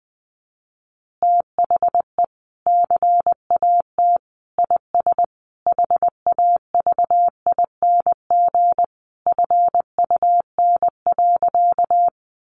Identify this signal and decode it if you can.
Morse code, heard as a keyed tone, about 20 wpm, 700 Hz. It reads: THE CAT IS HAVING FUN.
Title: Easy Phrases 20wpm